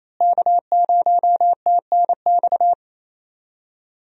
Morse code, heard as a keyed tone, about 27 wpm, 700 Hz. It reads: K0TNX
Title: USA Call Signs - 5 Characters - Repeated 3x - 28wpm